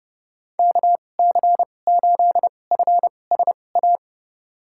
Morse code, heard as a keyed tone, about 30 wpm, spaced as for 22 wpm, 700 Hz. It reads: KC8FSA
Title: USA Call Signs - 6 Characters - 2x Character Spacing - 30wpm